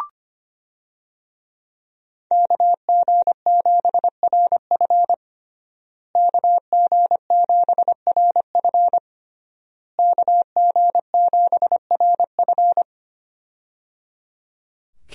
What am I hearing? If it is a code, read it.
Morse code, heard as a keyed tone, about 25 wpm, 700 Hz. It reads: KG7RF KG7RF KG7RF
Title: USA Call Signs - 5 Characters - Repeated 3x - 25wpm